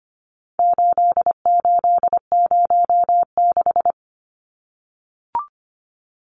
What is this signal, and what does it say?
Morse code, heard as a keyed tone, about 25 wpm, 700 Hz. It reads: 8806
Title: Numbers - 4-Digit 25wpm